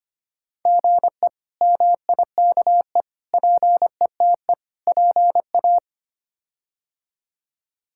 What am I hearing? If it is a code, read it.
Morse code, heard as a keyed tone, about 25 wpm, 700 Hz. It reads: GE MIKE PETE PA
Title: K1USN SST QSO Practice - 25wpm